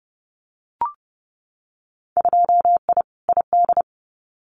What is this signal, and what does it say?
Morse code, heard as a keyed tone, about 30 wpm, 700 Hz. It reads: 2I ID
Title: Winter Field Day Exchanges Encore 30wpm